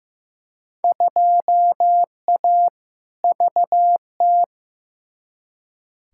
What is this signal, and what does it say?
Morse code, heard as a keyed tone, about 15 wpm, 700 Hz. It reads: 2A VT
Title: Field Day Exchanges Encore - 15wpm